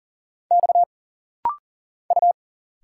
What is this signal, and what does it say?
Morse code, heard as a keyed tone, about 40 wpm, 700 Hz. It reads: X U
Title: Single Letters - Rapid-Fire 40wpm